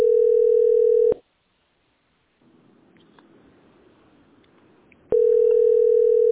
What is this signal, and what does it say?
Ringback tone (North America), 440 + 480 Hz.